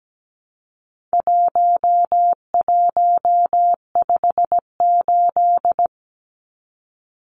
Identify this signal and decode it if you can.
Morse code, heard as a keyed tone, about 17 wpm, 700 Hz. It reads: 1158
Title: Numbers - 4-Digit 17wpm